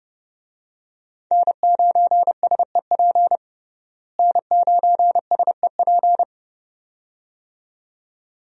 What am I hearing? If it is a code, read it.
Morse code, heard as a keyed tone, about 32 wpm, 700 Hz. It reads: N9SEP N9SEP
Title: USA Call Signs - 5 Characters - Repeated 2x - 30wpm